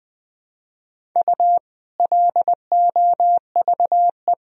Morse code, heard as a keyed tone, about 20 wpm, 700 Hz. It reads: U LOVE
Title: Sentences from Top 400 Words and Abbreviations 20wpm